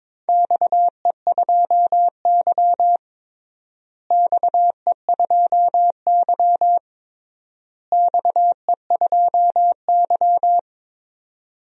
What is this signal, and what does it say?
Morse code, heard as a keyed tone, about 22 wpm, 700 Hz. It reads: XE2Y XE2Y XE2Y
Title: DXCC-100 Call Signs - 4 Characters - Repeated 3x - 22wpm